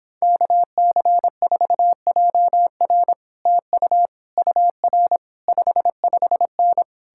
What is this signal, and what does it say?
Morse code, heard as a keyed tone, about 29 wpm, 700 Hz. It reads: KC4JR TU UR 55N